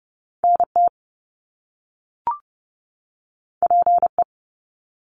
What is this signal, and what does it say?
Morse code, heard as a keyed tone, about 30 wpm, 700 Hz. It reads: NT PE
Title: Canadian Provinces and Territories 30wpm